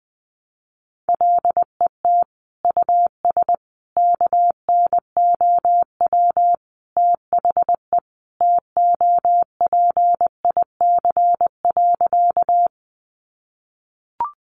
Morse code, heard as a keyed tone, about 20 wpm, 700 Hz. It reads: LET US KNOW THE TOPIC.
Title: Sentences from Top 1200 Words 20wpm